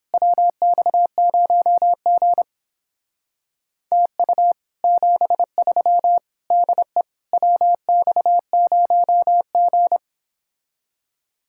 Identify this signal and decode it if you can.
Morse code, heard as a keyed tone, about 29 wpm, 700 Hz. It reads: WX0G TU 73 DE WX0G